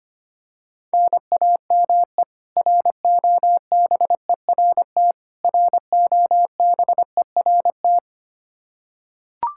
Morse code, heard as a keyed tone, about 25 wpm, 700 Hz. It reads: NAME ROBERT ROBERT
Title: Practice QSO - Exchange 1 - Beginning - 25wpm